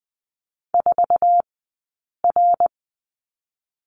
Morse code, heard as a keyed tone, about 20 wpm, spaced as for 8 wpm, 700 Hz. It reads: 4R
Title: 144 Lesson 24d - Letter y - Two Characters 20wpm